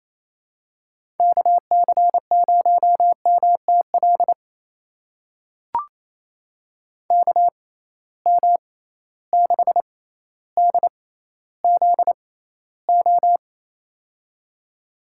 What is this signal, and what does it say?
Morse code, heard as a keyed tone, about 28 wpm, spaced as for 14 wpm, 700 Hz. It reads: KC0MTL KM6DZO
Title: USA Call Signs - 6 Characters - 6x Character Spacing - 28wpm